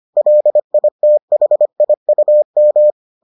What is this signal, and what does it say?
Morse code, heard as a keyed tone, about 25 wpm, 600 Hz. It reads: LITHIUM